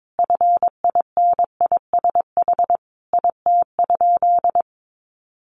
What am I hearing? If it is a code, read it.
Morse code, heard as a keyed tone, about 22 wpm, 700 Hz. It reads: FINISH IT?
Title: Sentences from Top 1200 Words 22wpm